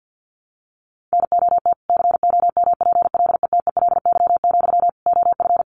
Morse code, heard as a keyed tone, about 50 wpm, 700 Hz. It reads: NOT COMPLETELY, OF
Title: Sentences from Top 1100 Words 50wpm